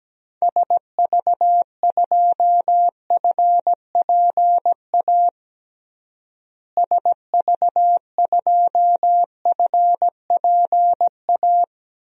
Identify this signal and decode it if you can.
Morse code, heard as a keyed tone, about 17 wpm, 700 Hz. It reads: SV2FPA SV2FPA
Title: DXCC-100 Call Signs - 6 Characters - Repeated 3x - 17wpm